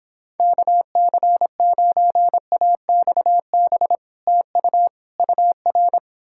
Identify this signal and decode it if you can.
Morse code, heard as a keyed tone, about 29 wpm, 700 Hz. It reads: KC9AXB TU UR